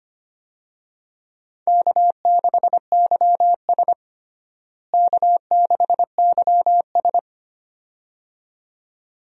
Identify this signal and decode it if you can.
Morse code, heard as a keyed tone, about 25 wpm, 700 Hz. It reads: K6YS K6YS
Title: USA Call Signs - 4 Characters - Repeated 2x - 25wpm